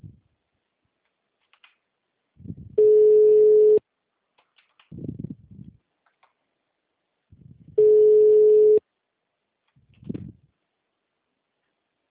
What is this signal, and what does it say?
Ringback tone (Europe, 425 Hz).